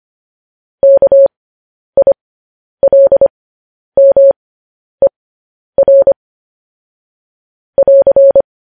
Morse code, heard as a keyed tone, about 25 wpm, spaced as for 10 wpm, 550 Hz. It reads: KILMER <AR>
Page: news headlines in morse code at 10 wpm